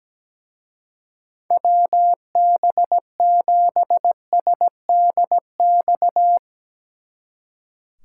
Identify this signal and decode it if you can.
Morse code, heard as a keyed tone, about 18 wpm, 700 Hz. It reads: WB7SDX
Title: Speed Racing - USA 2x3 Call Signs 17wpm